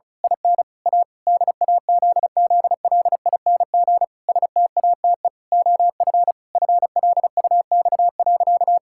Morse code, heard as a keyed tone, about 36 wpm, 700 Hz. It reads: IN A DAZZLING STATE OF FLUX.
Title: Pangram Challenge v2 - 35wpm